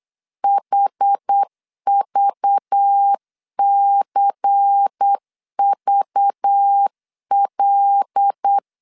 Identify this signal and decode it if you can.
Morse code, heard as a keyed tone, about 8 wpm, 800 Hz. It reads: HVCVL
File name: 8wpm